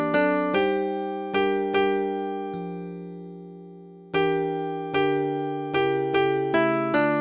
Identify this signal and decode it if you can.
There is no code.